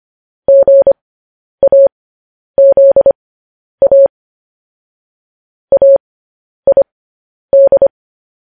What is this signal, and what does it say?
Morse code, heard as a keyed tone, about 25 wpm, spaced as for 10 wpm, 550 Hz. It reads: GAZA AID
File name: podcast / News headlines-1x-25x10